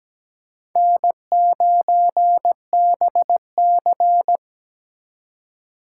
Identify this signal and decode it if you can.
Morse code, heard as a keyed tone, about 20 wpm, 700 Hz. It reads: N9BC
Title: Speed Racing - USA Call Signs 17wpm